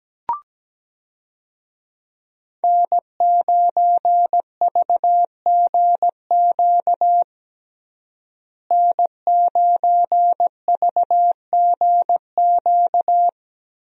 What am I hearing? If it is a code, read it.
Morse code, heard as a keyed tone, about 17 wpm, 700 Hz. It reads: N9VGQ N9VGQ
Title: USA Call Signs - 5 Characters - Repeated 2x - 17wpm